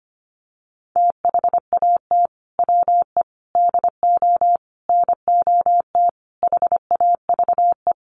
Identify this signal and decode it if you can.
Morse code, heard as a keyed tone, about 25 wpm, 700 Hz. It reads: THAT WE DO NOT HAVE
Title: Sets of 5 Words Encore 25wpm